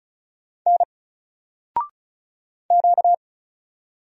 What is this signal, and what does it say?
Morse code, heard as a keyed tone, about 34 wpm, 700 Hz. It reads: N Q
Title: Single Character (Letter:Number:Punctuation:Prosign) 35wpm